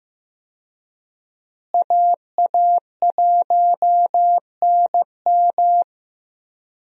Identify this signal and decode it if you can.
Morse code, heard as a keyed tone, about 15 wpm, 700 Hz. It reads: AA1NM
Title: USA Call Signs - 5 Characters - Repeated 3x - 15wpm